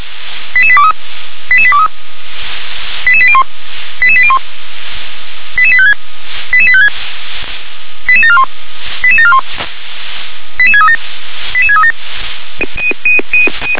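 Five-tone selective call (ZVEI), heard as quick runs of five tones, tones of about 70 ms. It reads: 88823 88823 88801 88801 88856 88856 88741 88741 88637 88637
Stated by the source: ZVEI five-tone call 88774 received on an SDR, channel 357.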